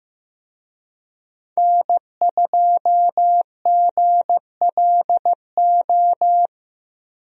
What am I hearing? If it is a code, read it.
Morse code, heard as a keyed tone, about 15 wpm, 700 Hz. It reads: N2GLO
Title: USA Call Signs - 5 Characters - Repeated 2x - 15wpm